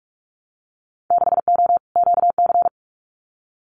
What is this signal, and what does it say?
Morse code, heard as a keyed tone, about 46 wpm, 700 Hz. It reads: 6O QC
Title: Winter Field Day Exchanges Encore 45wpm